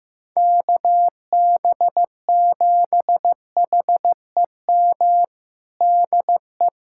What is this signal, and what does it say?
Morse code, heard as a keyed tone, about 15 wpm, 700 Hz. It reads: KB7HEM DE
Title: USA Callsign DE Callsign - 6 Characters - 15wpm